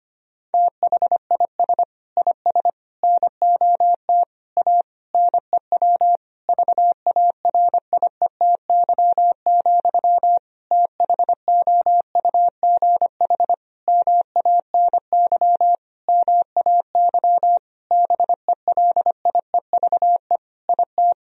Morse code, heard as a keyed tone, about 25 wpm, 700 Hz. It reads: THIS IS NOT A NEW VARIETY, THOUGH MANY MAY BELIEVE IT
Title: Sentences from Top 1000 Words - Review Part 4 - 25wpm